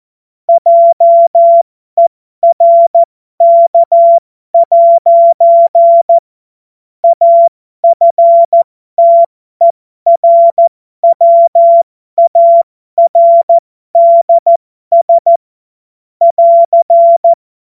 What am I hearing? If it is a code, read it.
Morse code, heard as a keyed tone, about 14 wpm, spaced as for 12 wpm, 680 Hz. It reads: JERK' AFTERWARDS <AR>